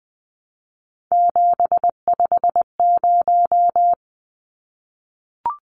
Morse code, heard as a keyed tone, about 20 wpm, 700 Hz. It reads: 750
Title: Numbers - 3-Digit 20wpm